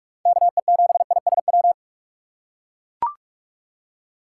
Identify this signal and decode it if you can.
Morse code, heard as a keyed tone, about 41 wpm, 700 Hz. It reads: KE7ISW
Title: USA Call Signs - 6x Character Spacing - 45wpm